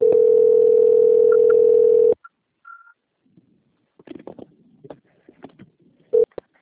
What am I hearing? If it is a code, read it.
Ringback tone (North America), 440 + 480 Hz.